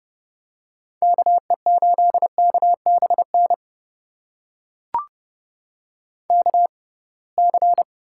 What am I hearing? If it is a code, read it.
Morse code, heard as a keyed tone, about 31 wpm, spaced as for 23 wpm, 700 Hz. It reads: KE8KBN KC
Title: USA Call Signs - 6x Character Spacing - 30wpm